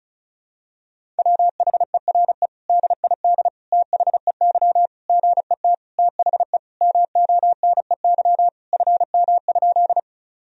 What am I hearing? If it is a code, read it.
Morse code, heard as a keyed tone, about 36 wpm, 700 Hz. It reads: WHERE DID THEY GET THE MONEY FM?
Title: Sentences from Top 400 Words and Abbreviations 35wpm